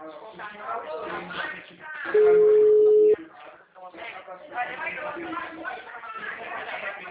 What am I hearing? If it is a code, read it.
Ringback tone (Europe, 425 Hz), with speech.
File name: out- -2012- - - .87838